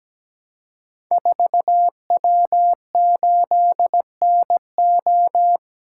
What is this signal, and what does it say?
Morse code, heard as a keyed tone, about 17 wpm, 700 Hz. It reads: 4W8NO
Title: Code Groups - 5 Letter:Numbers - 17wpm